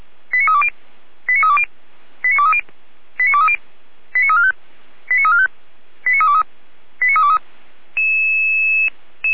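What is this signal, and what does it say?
Five-tone selective call (ZVEI), heard as quick runs of five tones, tones of about 80 ms. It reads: 89320 89320 89230 89230 89345 89345 89323 89323
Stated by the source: ZVEI five-tone selective call 89001 received by SDR, fire brigade Feuerwehr Orlamünde.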